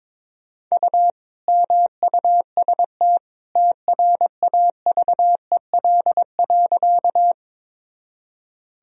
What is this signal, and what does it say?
Morse code, heard as a keyed tone, about 22 wpm, 700 Hz. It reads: U MUST TRAVEL.
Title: Sentences from Top 400 Words and Abbreviations 22wpm